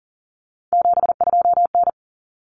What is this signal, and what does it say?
Morse code, heard as a keyed tone, about 40 wpm, 700 Hz. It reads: 72N
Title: Common Cut Numbers - 3-Digit 40wpm